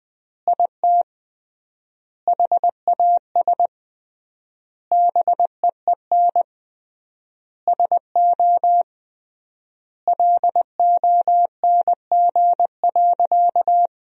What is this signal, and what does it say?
Morse code, heard as a keyed tone, about 20 wpm, 700 Hz. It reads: IT HAS BEEN SO LONG.
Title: 3x Word-Spacing - Sentences from Top 100 Words - 20wpm